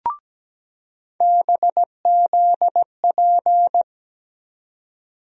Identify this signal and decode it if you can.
Morse code, heard as a keyed tone, about 17 wpm, 700 Hz. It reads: BZP
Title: USA Call Sign Postfixes - 3 Letters 17wpm